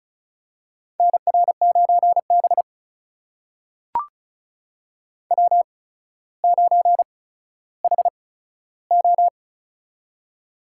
Morse code, heard as a keyed tone, about 36 wpm, spaced as for 15 wpm, 700 Hz. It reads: NR9B W9HO
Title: USA Call Signs - 4 Characters - 8x Character Spacing - 35wpm